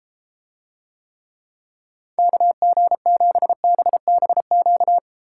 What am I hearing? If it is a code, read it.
Morse code, heard as a keyed tone, about 32 wpm, 700 Hz. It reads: KG7BBQ